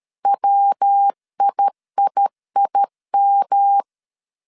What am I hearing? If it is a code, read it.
Morse code, heard as a keyed tone, about 13 wpm, 800 Hz. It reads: WIIIM
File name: CW / DET / 1 3 1500C S C 12wpm 2750